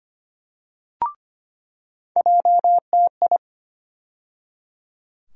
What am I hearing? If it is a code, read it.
Morse code, heard as a keyed tone, about 25 wpm, 700 Hz. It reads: JTI
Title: USA Call Sign Postfixes - 3 Letters 25wpm